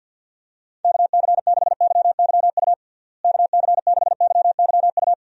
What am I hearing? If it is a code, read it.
Morse code, heard as a keyed tone, about 49 wpm, 700 Hz. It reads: KC6YYU KC6YYU
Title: USA Call Signs - 6 Characters - Repeated 2x - 50wpm